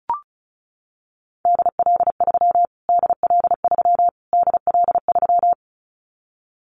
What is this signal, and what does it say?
Morse code, heard as a keyed tone, about 34 wpm, 700 Hz. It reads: DL3 DL3 DL3
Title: Callsign Prefixes for Top 25 DXCC - Repeated 3x - 35wpm